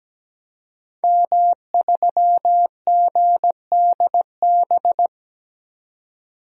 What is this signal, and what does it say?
Morse code, heard as a keyed tone, about 17 wpm, 700 Hz. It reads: M3GDB
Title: DXCC-100 Call Signs - 5 Characters - 17wpm